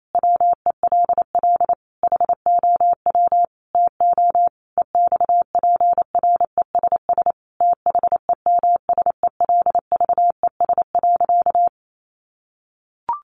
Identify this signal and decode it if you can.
Morse code, heard as a keyed tone, about 28 wpm, 700 Hz. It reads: WELL HOW TO EXPRESS THEMSELVES.
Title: Sentences from Top 1100 Words 28wpm